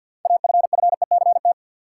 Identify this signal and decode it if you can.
Morse code, heard as a keyed tone, about 52 wpm, 700 Hz. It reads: AFFECT
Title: Sentences from Top 900 Words - New Word Drill - 50wpm